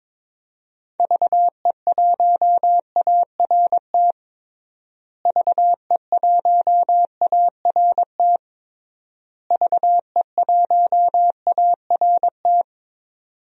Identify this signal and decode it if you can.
Morse code, heard as a keyed tone, about 22 wpm, 700 Hz. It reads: VE1ART VE1ART VE1ART
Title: DXCC-100 Call Signs - 6 Characters - Repeated 3x - 22wpm